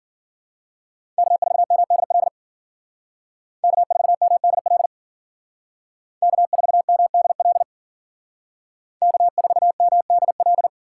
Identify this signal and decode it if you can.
Morse code, heard as a keyed tone, about 50 wpm, 700 Hz. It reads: K4MDL K4MDL K4MDL K4MDL